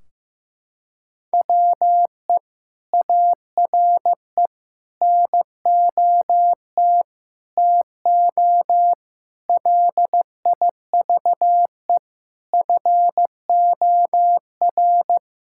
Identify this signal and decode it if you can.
Morse code, heard as a keyed tone, about 16 wpm, 700 Hz. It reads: WE ARE NOT TO LIVE FOR
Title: Sentences from Top 1600 Words 15wpm